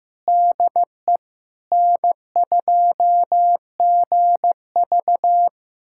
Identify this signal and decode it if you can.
Morse code, heard as a keyed tone, about 16 wpm, 700 Hz. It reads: DE N2GV
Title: USA Callsign DE Callsign - 4 Characters - 15wpm